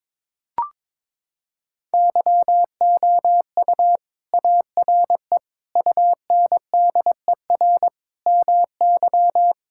Morse code, heard as a keyed tone, about 21 wpm, 700 Hz. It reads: YOU ARE UNDER MY
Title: Sentences from Top 1600 Words 22wpm